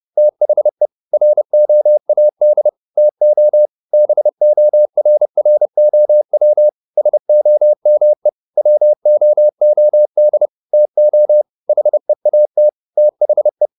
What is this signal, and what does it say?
Morse code, heard as a keyed tone, about 30 wpm, 600 Hz. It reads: THE ROAD TO BORROW SOME WOOD TO HEAT THE